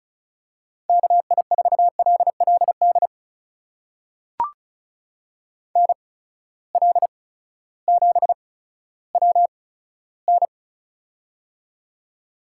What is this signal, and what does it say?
Morse code, heard as a keyed tone, about 36 wpm, spaced as for 16 wpm, 700 Hz. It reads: KI4LLD NL7WN
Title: USA Call Signs - 8x Character Spacing - 35wpm